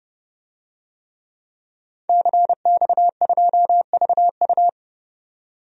Morse code, heard as a keyed tone, about 32 wpm, 700 Hz. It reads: CX2VU
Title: DXCC-100 Call Signs - 5 Characters - 30wpm